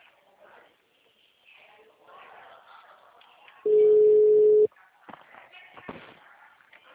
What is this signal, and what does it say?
Ringback tone (Europe, 425 Hz), with speech.
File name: out- -2020